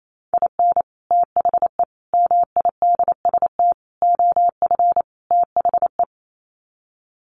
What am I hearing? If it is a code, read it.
Morse code, heard as a keyed tone, about 28 wpm, 700 Hz. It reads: IN THE MIDST OF THE
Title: Sets of 5 Words 28wpm